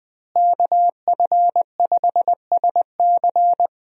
Morse code, heard as a keyed tone, about 20 wpm, 700 Hz. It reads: KF5SC
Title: USA Call Signs - 5 Characters - Repeated 2x - 20wpm